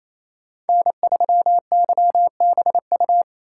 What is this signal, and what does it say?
Morse code, heard as a keyed tone, about 28 wpm, 700 Hz. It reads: N3YBU
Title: USA Call Signs - 5 Characters - 4x Character Spacing - 28wpm